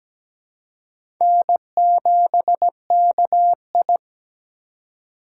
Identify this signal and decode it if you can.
Morse code, heard as a keyed tone, about 17 wpm, 700 Hz. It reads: N7KI
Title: USA Call Signs - 4 Characters - 3x Character Spacing - 17wpm